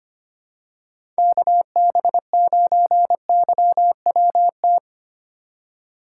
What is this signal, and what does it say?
Morse code, heard as a keyed tone, about 25 wpm, 700 Hz. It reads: KB9YWT
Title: USA Call Signs - 2x Character Spacing - 25wpm